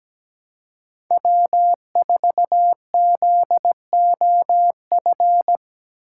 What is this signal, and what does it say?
Morse code, heard as a keyed tone, about 17 wpm, 700 Hz. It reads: W4ZOF
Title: Code Groups - 5 Letter:Numbers - 17wpm